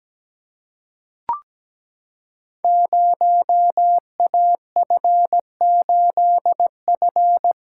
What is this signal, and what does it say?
Morse code, heard as a keyed tone, about 17 wpm, 700 Hz. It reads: 0AF8F
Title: Code Groups - 5 Letter:Numbers - 17wpm